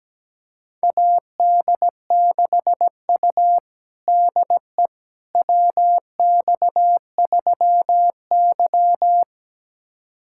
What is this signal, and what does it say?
Morse code, heard as a keyed tone, about 17 wpm, 700 Hz. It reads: AD6U DE WX3Y
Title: USA Callsign DE Callsign - 4 Characters - 17wpm